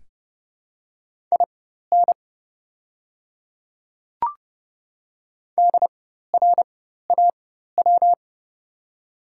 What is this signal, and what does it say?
Morse code, heard as a keyed tone, about 30 wpm, spaced as for 14 wpm, 700 Hz. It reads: IN DRAW
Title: 4x Character Spacing - Top 300 Words - 30wpm